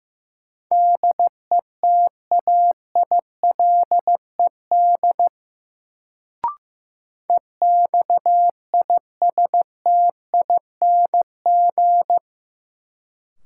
Morse code, heard as a keyed tone, about 15 wpm, 700 Hz. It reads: DETAILED EXISTING
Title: Sentences from Top 1500 Words - New Word Drill - 15wpm